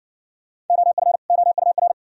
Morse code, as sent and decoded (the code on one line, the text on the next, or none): --- ..-. / --- ..- .-.
OF OUR